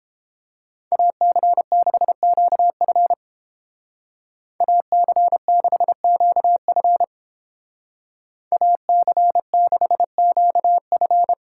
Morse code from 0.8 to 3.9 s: AC6QF

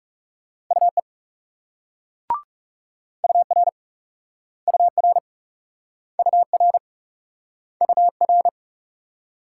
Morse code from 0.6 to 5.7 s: UE UR UR

